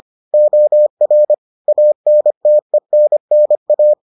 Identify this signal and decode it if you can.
Morse code, heard as a keyed tone, about 25 wpm, 600 Hz. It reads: OR ANTENNA